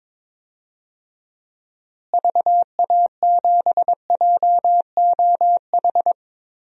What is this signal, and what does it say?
Morse code, heard as a keyed tone, about 22 wpm, 700 Hz. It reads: VA7JOH
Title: DXCC-100 Call Signs - 6 Characters - Repeated 2x - 22wpm